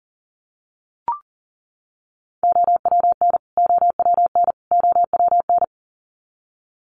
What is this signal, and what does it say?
Morse code, heard as a keyed tone, about 40 wpm, 700 Hz. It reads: OWN OWN OWN
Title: Top 500 Words Repeated 40wpm